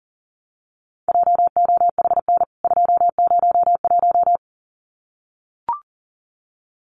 Morse code, heard as a keyed tone, about 40 wpm, 700 Hz. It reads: JOHN 201
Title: CWT Exchange 40wpm